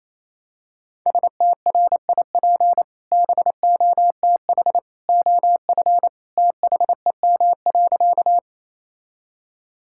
Morse code, heard as a keyed tone, about 28 wpm, 700 Hz. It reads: STRIP BOTH OF THEM.
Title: Sentences from Top 1600 Words 28wpm